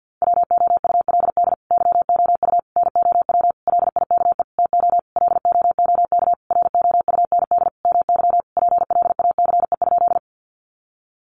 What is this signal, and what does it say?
Morse code, heard as a keyed tone, about 48 wpm, 700 Hz. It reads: WOULD YOU NOW LIKE TO LOOK ROUND MY PLACE?
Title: Sentences from Top 200 Words 50wpm